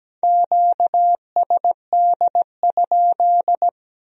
Morse code, heard as a keyed tone, about 20 wpm, 700 Hz. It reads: QSD?
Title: Speed-Racing Q-Codes - 17wpm